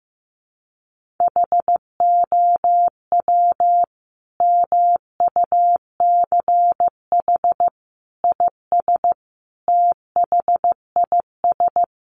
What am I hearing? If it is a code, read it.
Morse code, heard as a keyed tone, about 15 wpm, 700 Hz. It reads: HOW MUCH IS THIS
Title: Sentences from Top 1200 Words 15wpm